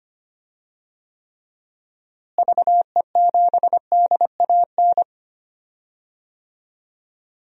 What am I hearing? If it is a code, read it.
Morse code, heard as a keyed tone, about 25 wpm, 700 Hz. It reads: VE7DAN
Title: DXCC-100 Call Signs - 6 Characters - Repeated 2x - 25wpm